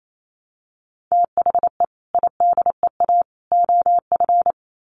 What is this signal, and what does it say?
Morse code, heard as a keyed tone, about 28 wpm, 700 Hz. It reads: THE IDEA OF
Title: Sets of 3 Words 28wpm